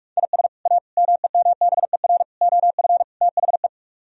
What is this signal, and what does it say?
Morse code, heard as a keyed tone, about 44 wpm, 700 Hz. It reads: IS A MEMBER OF THE